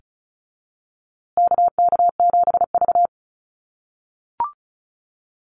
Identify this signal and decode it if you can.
Morse code, heard as a keyed tone, about 34 wpm, 700 Hz. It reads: KK7V
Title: USA Call Signs Repeated 2x 35wpm